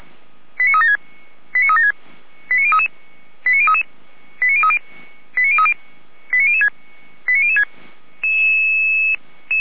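Five-tone selective call (ZVEI), heard as quick runs of five tones, tones of about 70 ms. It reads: 89376 89376 89033 89033 89030 89030 89006 89006
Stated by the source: ZVEI five-tone call received by SDR, logged by feuerwehr-orlamuende (Orlamünde fire brigade).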